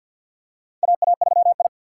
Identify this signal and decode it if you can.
Morse code, heard as a keyed tone, about 50 wpm, 700 Hz. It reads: AA3I
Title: USA Call Signs - 4 Characters - 6x Character Spacing - 50wpm